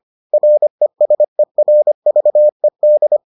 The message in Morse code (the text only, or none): RESERVED